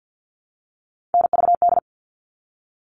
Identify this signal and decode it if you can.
Morse code, heard as a keyed tone, about 50 wpm, 700 Hz. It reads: NVD
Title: All Cut Numbers - 3-Digit 50wpm